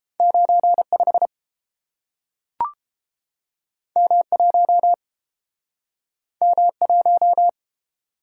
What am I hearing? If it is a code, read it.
Morse code, heard as a keyed tone, about 30 wpm, 700 Hz. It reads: 95 M1 M1